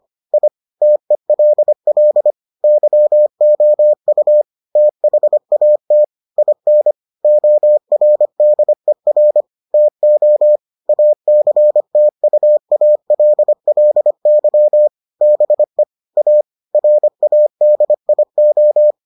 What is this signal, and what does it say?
Morse code, heard as a keyed tone, about 25 wpm, 600 Hz. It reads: I TELL YOU THAT IN ORDER TO ACTUALLY BE A RADIO